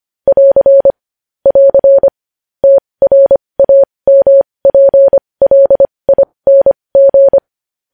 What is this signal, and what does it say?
Morse code, heard as a keyed tone, about 25 wpm, spaced as for 20 wpm, 550 Hz. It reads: <AR> <AR> TRAMPLING
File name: podcast / News headlines-1x-25x20